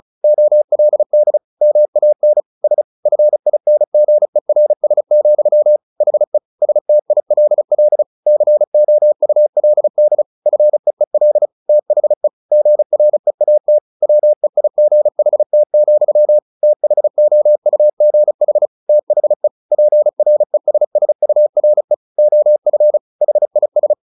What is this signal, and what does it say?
Morse code, heard as a keyed tone, about 34 wpm, 600 Hz. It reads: OLD MAN S FINGERS, HE STILL COULD FEEL THE GREAT WEIGHT, THOUGH THE PRESSURE OF HIS
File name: the old man and the sea / the old man and the sea 1 35wpm